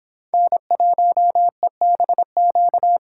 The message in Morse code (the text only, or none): N1EBQ